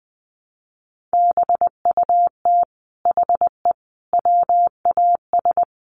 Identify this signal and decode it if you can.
Morse code, heard as a keyed tone, about 20 wpm, 700 Hz. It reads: BUT HE WAS
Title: Sentences from Top 700 Words 20wpm